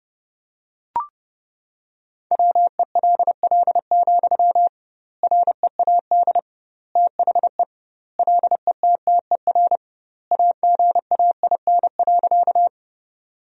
Morse code, heard as a keyed tone, about 30 wpm, 700 Hz. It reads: WELL, READ THE LETTER AGAIN.
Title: 2x Word-Spacing - Sentences from Top 300 Words - 30wpm